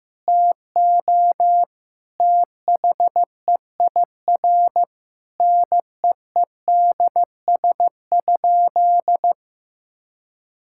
Morse code, heard as a keyed tone, about 15 wpm, 700 Hz. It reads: TO THEIR NEEDS?
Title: Sentences from Top 1000 Words - Review Part 5 - 15wpm